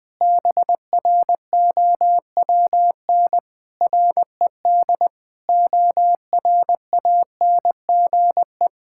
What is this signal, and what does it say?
Morse code, heard as a keyed tone, about 19 wpm, 700 Hz. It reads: BROWN RED ORANGE
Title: Pangram Challenge v2 - 20wpm